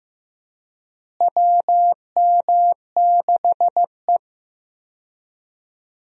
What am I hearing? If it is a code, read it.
Morse code, heard as a keyed tone, about 15 wpm, 700 Hz. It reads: WM6E
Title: USA Call Signs - 4 Characters - Repeated 2x - 15wpm